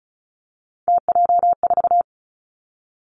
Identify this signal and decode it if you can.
Morse code, heard as a keyed tone, about 36 wpm, 700 Hz. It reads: TJ4
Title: Three Challenging Letter-Number Combinations 35wpm